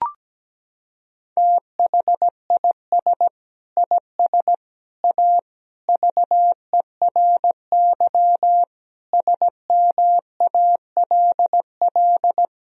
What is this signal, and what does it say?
Morse code, heard as a keyed tone, about 17 wpm, 700 Hz. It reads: THIS IS A VERY SMALL
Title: Sentences from Top 1000 Words - Review Part 1 - 17wpm